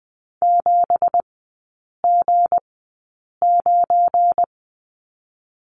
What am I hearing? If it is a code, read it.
Morse code, heard as a keyed tone, about 19 wpm, spaced as for 8 wpm, 700 Hz. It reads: 7G9
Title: 172 Lesson 28e - Number 9 - Three Characters 20wpm